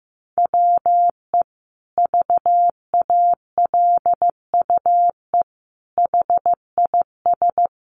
Morse code, heard as a keyed tone, about 15 wpm, 700 Hz. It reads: WE VALUE HIS
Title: Sentences from Top 1100 Words 15wpm